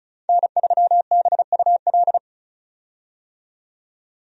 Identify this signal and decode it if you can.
Morse code, heard as a keyed tone, about 36 wpm, 700 Hz. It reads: N3BUL
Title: USA Call Signs - 5 Characters - 35wpm